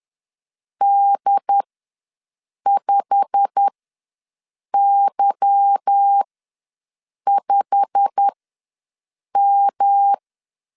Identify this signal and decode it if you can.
Morse code, heard as a keyed tone, about 10 wpm, spaced as for 6 wpm, 800 Hz. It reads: D5Y5M
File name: LEZIONI / 8 YJ45 / 8 YJ45 ws 10wpm 3843